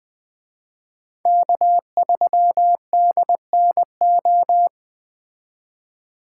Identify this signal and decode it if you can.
Morse code, heard as a keyed tone, about 20 wpm, 700 Hz. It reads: K3DNO
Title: Speed Racing - USA 1x3 Call Signs 17wpm